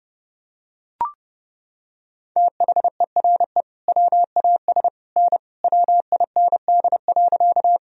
Morse code, heard as a keyed tone, about 30 wpm, 700 Hz. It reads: THERE WAS N WIND.
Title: Sentences from Top 400 Words and Abbreviations 30wpm